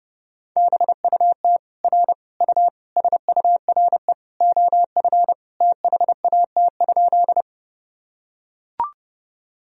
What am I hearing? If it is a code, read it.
Morse code, heard as a keyed tone, about 32 wpm, 700 Hz. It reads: BUT R U SURE OF THAT?
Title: Sentences from Top 300 Words and Abbreviations 30wpm